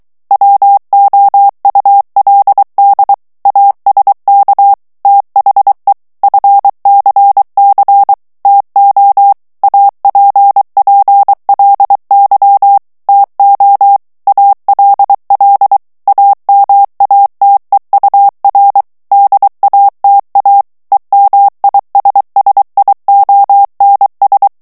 Morse code, heard as a keyed tone, about 23 wpm, 800 Hz. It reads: WOULD ASK THE FCC TO APPLY TO ALL AMATEUR DATA EMISSIONS
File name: ANGLAIS texte clair 3 vitesse 22 mots minute